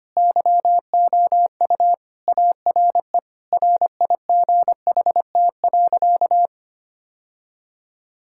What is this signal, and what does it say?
Morse code, heard as a keyed tone, about 25 wpm, 700 Hz. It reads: YOU ARE RIGHT.
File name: practice / 25wpm / Sentences with 3-Word Phrases - 25wpm